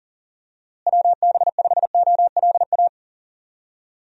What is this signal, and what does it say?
Morse code, heard as a keyed tone, about 40 wpm, 700 Hz. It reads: WB5OLA
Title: USA Call Signs - 6 Characters - Repeated 2x - 40wpm